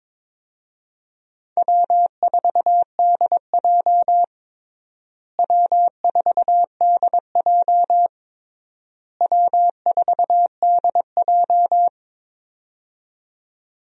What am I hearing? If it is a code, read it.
Morse code, heard as a keyed tone, about 22 wpm, 700 Hz. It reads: W4DJ W4DJ W4DJ
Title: USA Call Signs - 4 Characters - Repeated 3x - 22wpm